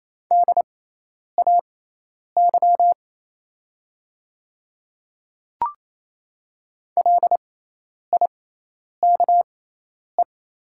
Morse code, heard as a keyed tone, about 28 wpm, spaced as for 10 wpm, 700 Hz. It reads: DAY LIKE
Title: 6x Character Spacing - Top 100 Words - 28wpm